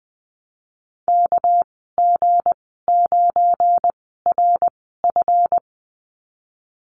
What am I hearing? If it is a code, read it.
Morse code, heard as a keyed tone, about 20 wpm, spaced as for 15 wpm, 700 Hz. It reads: KG9RF
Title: 231 Lesson 37g - Letter x - Call Signs 20wpm